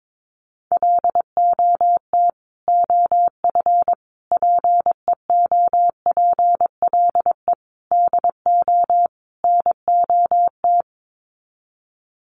Morse code, heard as a keyed tone, about 22 wpm, 700 Hz. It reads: LOT OF PEOPLE DO NOT
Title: Sets of 5 Words Encore 22wpm